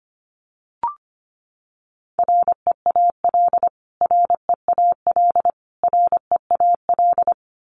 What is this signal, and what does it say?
Morse code, heard as a keyed tone, about 25 wpm, 700 Hz. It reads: REAL REAL REAL
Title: Top 500 Words Repeated 25wpm